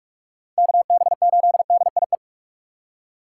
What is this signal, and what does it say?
Morse code, heard as a keyed tone, about 46 wpm, 700 Hz. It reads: KB8DIE
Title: USA Call Signs - 4x Character Spacing - 45wpm